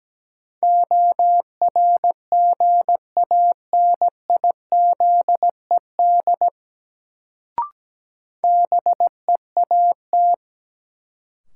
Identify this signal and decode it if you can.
Morse code, heard as a keyed tone, about 17 wpm, 700 Hz. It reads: ORGANIZED BEAT
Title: Sentences from Top 1300 Words - New Word Drill Alt - 17wpm